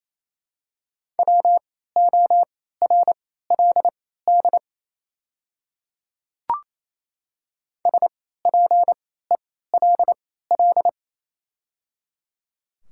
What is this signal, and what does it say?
Morse code, heard as a keyed tone, about 27 wpm, spaced as for 16 wpm, 700 Hz. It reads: WORLD SPELL
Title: 3x Character Spacing - Top 200 Words - 28wpm